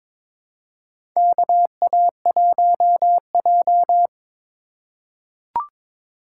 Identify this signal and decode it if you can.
Morse code, heard as a keyed tone, about 22 wpm, 700 Hz. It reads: KA1J
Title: USA Call Signs - 4 Characters - 4x Character Spacing - 22wpm